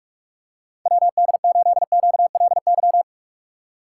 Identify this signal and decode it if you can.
Morse code, heard as a keyed tone, about 44 wpm, 700 Hz. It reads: WD8QLY